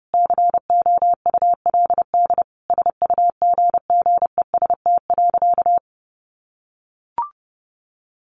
Morse code, heard as a keyed tone, about 30 wpm, 700 Hz. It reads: COULD SUGGEST.
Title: Sentences from Top 1500 Words 30wpm